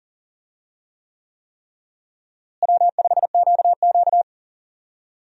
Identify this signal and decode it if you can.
Morse code, heard as a keyed tone, about 39 wpm, 700 Hz. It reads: W5QQ
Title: USA Call Signs - 4 Characters - Repeated 3x - 40wpm